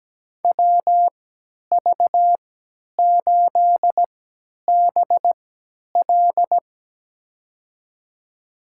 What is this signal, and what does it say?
Morse code, heard as a keyed tone, about 17 wpm, spaced as for 10 wpm, 700 Hz. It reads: WV8BL
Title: USA Call Signs - 5 Characters - 3x Character Spacing - 17wpm